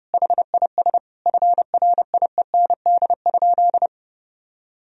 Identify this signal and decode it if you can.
Morse code, heard as a keyed tone, about 32 wpm, 700 Hz. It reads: HIS FRIEND?
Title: Sentences from Top 1000 Words - Review Part 3 - 30wpm